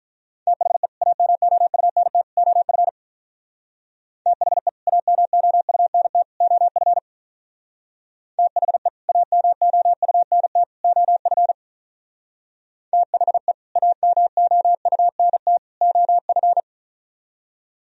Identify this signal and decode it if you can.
Morse code, heard as a keyed tone, about 45 wpm, 700 Hz. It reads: THE AMOUNT OF THE AMOUNT OF THE AMOUNT OF THE AMOUNT OF